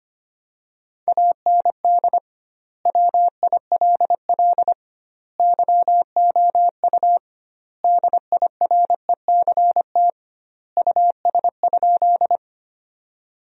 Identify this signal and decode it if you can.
Morse code, heard as a keyed tone, about 25 wpm, 700 Hz. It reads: AND WILL YOU DIRECT US?
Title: 2x Word-Spacing - Sentences from Top 400 Words - 25wpm